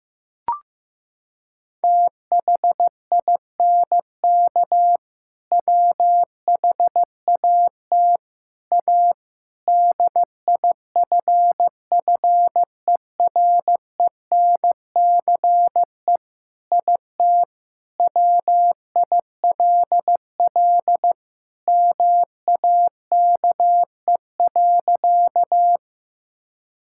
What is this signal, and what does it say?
Morse code, heard as a keyed tone, about 15 wpm, 700 Hz. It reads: THINK WHAT A DIFFERENCE IT WILL MAKE.
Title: Sentences from Top 1000 Words - Review Part 5 - 15wpm